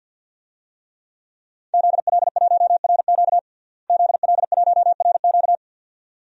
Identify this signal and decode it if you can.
Morse code, heard as a keyed tone, about 45 wpm, 700 Hz. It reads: ZL1RQ ZL1RQ
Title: DXCC-100 Call Signs - 5 Characters - Repeated 2x - 50wpm